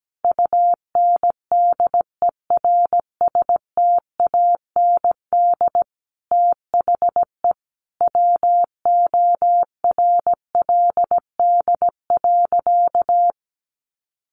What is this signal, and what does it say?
Morse code, heard as a keyed tone, about 17 wpm, 700 Hz. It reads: UNDERSTAND THE WORLD.
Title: Sentences from Top 600 Words 17wpm